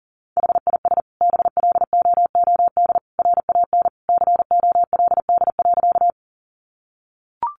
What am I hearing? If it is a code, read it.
Morse code, heard as a keyed tone, about 37 wpm, 700 Hz. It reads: HIS BLOOD RAN COLD.
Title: Sentences from Top 1100 Words 40wpm